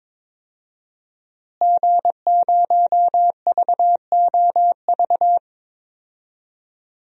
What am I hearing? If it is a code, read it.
Morse code, heard as a keyed tone, about 22 wpm, 700 Hz. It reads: G0VOV
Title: DXCC-100 Call Signs - 5 Characters - Repeated 2x - 22wpm